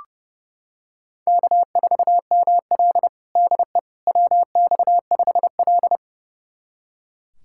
Morse code, heard as a keyed tone, about 32 wpm, 700 Hz. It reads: K4ML DE WX5L
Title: USA Callsign DE Callsign - 4 Characters - 30wpm